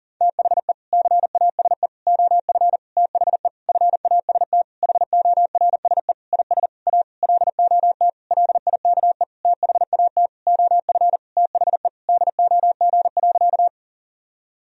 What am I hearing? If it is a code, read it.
Morse code, heard as a keyed tone, about 40 wpm, 700 Hz. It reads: THE CASE OF THE FAST HORSE IS A LOT LIKE THAT OF THE DOG.
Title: Sentences from Top 1000 Words - Review Part 2 - 40wpm